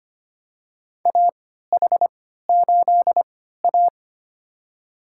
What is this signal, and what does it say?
Morse code, heard as a keyed tone, about 25 wpm, spaced as for 14 wpm, 700 Hz. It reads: AH8A
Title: USA Call Signs - 3x Character Spacing - 25wpm